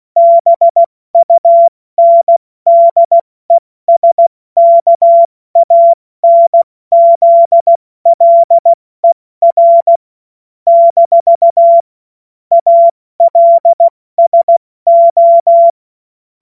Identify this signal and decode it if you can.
Morse code, heard as a keyed tone, about 16 wpm, 680 Hz. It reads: BUNDESKANZLER - ALSO
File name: Witz-30-September-2025-70-F80